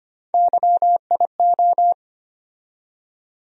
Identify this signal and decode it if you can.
Morse code, heard as a keyed tone, about 25 wpm, 700 Hz. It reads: YIO
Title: USA Call Sign Postfixes - 3 Letters 25wpm